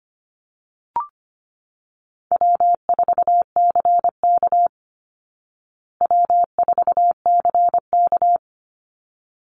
Morse code, heard as a keyed tone, about 25 wpm, 700 Hz. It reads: W4CK W4CK
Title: Easy Call Signs Repeated 2x - 25wpm